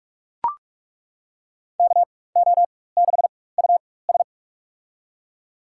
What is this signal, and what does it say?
Morse code, heard as a keyed tone, about 41 wpm, spaced as for 20 wpm, 700 Hz. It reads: KO6US
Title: USA Call Signs - 4x Character Spacing - 45wpm